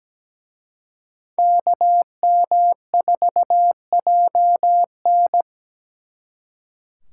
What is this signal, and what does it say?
Morse code, heard as a keyed tone, about 20 wpm, 700 Hz. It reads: KM4JN
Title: Speed Racing - USA 2x2 Call Signs 17wpm